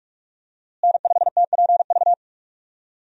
Morse code, heard as a keyed tone, about 42 wpm, 700 Hz. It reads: N5TPV
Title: USA Call Signs - 5 Characters - Repeated 2x - 45wpm